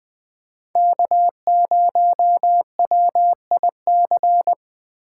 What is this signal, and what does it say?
Morse code, heard as a keyed tone, about 20 wpm, 700 Hz. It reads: K0WIC
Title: USA Call Signs - 5 Characters - 3x Character Spacing - 20wpm